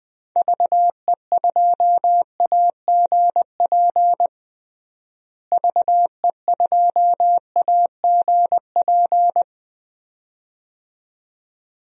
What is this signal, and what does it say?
Morse code, heard as a keyed tone, about 21 wpm, 700 Hz. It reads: VE2AGP VE2AGP
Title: DXCC-100 Call Signs - 6 Characters - Repeated 2x - 20wpm